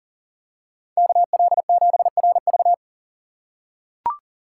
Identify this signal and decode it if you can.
Morse code, heard as a keyed tone, about 40 wpm, 700 Hz. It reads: KL7RV
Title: USA Call Signs - 3x Character Spacing - 40wpm